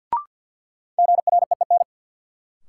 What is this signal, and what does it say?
Morse code, heard as a keyed tone, about 53 wpm, 700 Hz. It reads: GREEN
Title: Sentences from Top 500 Words - New Word Drill - 50wpm